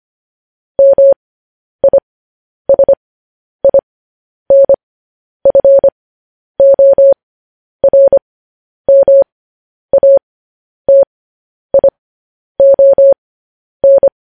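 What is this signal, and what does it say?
Morse code, heard as a keyed tone, about 25 wpm, spaced as for 10 wpm, 550 Hz. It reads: MISINFORMATION
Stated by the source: Morse code news headlines broadcast at 10 wpm.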